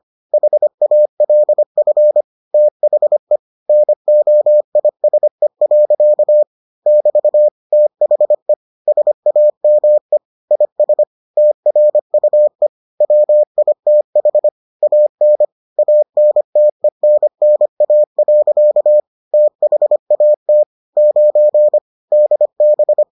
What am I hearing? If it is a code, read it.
Morse code, heard as a keyed tone, about 25 wpm, 600 Hz. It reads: HALF THE NOISE. <BT> THE SAME IS TRUE WITH AN ANTENNA. THAT 9 DB